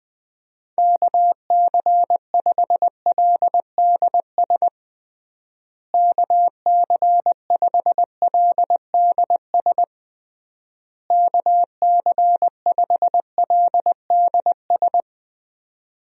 Morse code, heard as a keyed tone, about 20 wpm, 700 Hz. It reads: KC5LDS KC5LDS KC5LDS
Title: USA Call Signs - 6 Characters - Repeated 3x - 20wpm